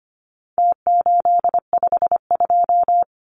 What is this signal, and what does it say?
Morse code, heard as a keyed tone, about 25 wpm, 700 Hz. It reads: T852
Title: Numbers - Contest Style - 4-Digit 25wpm